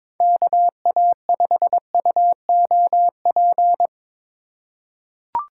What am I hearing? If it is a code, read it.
Morse code, heard as a keyed tone, about 22 wpm, 700 Hz. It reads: KA5UOP
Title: USA Call Signs - 6 Characters - 2x Character Spacing - 22wpm